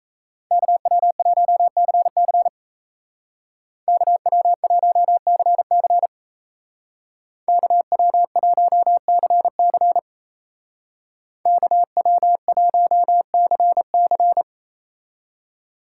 Morse code, heard as a keyed tone, about 33 wpm, 700 Hz. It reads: KW1CC KW1CC KW1CC KW1CC
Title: Speed Racing - USA 2x2 Call Signs 28wpm